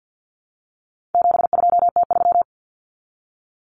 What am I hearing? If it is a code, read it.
Morse code, heard as a keyed tone, about 50 wpm, 700 Hz. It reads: 72T3
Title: Common Cut Numbers - 4-Digit 50wpm